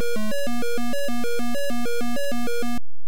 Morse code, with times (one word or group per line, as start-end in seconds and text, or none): none